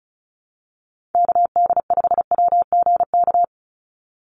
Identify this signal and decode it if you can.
Morse code, heard as a keyed tone, about 34 wpm, 700 Hz. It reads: KD5WGK
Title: USA Call Signs 35wpm